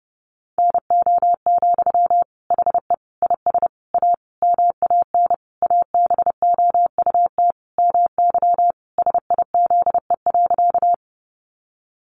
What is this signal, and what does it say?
Morse code, heard as a keyed tone, about 30 wpm, 700 Hz. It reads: NO, HE IS A MAN ABOUT MY SIZE.
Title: Sentences from Top 400 Words 30wpm